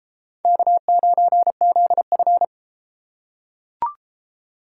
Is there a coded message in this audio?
Yes: Morse code from 0.5 to 2.4 s, heard as a keyed tone, about 33 wpm, 700 Hz. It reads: K9ZF